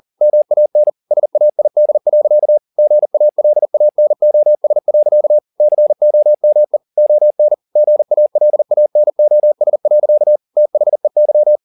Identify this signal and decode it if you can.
Morse code, heard as a keyed tone, about 40 wpm, 600 Hz. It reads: MAN SAID. GALANOS. COME ON GALANOS. THEY
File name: the old man and the sea 2 40wpm